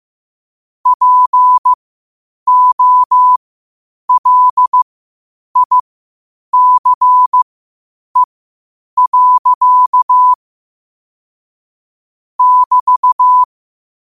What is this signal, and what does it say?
Morse code, heard as a keyed tone, about 15 wpm, spaced as for 8 wpm, 1000 Hz. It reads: POLICE. <BT>
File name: qotdfiles / 8wpm